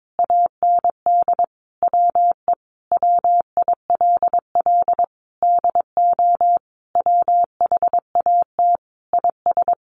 Morse code, heard as a keyed tone, about 22 wpm, 700 Hz. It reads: AND WE WILL DO WHAT IS
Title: Sentences from Top 900 Words 22wpm